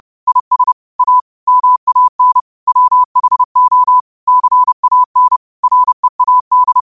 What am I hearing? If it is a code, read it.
Morse code, heard as a keyed tone, about 30 wpm, 1000 Hz. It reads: IS A MAN WHO CAN READ
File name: qotdfiles / 30wpm